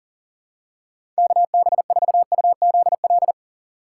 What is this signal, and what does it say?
Morse code, heard as a keyed tone, about 37 wpm, 700 Hz. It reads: KB4UZL
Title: USA Call Signs - 6 Characters - 2x Character Spacing - 40wpm